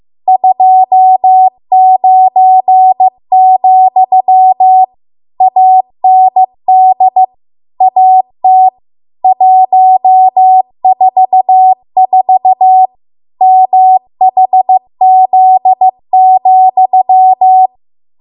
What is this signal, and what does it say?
Morse code, heard as a keyed tone, about 15 wpm, 750 Hz. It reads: 29, AND AT 144 MHZ,